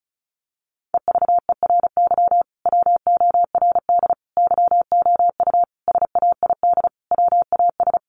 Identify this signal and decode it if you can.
Morse code, heard as a keyed tone, about 34 wpm, 700 Hz. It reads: EVERY WORD YOU SAID WAS
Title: Sentences from Top 400 Words 35wpm